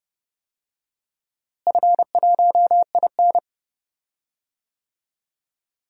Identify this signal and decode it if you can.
Morse code, heard as a keyed tone, about 30 wpm, 700 Hz. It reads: F1IN
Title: DXCC-100 Call Signs - 4 Characters - Repeated 2x - 30wpm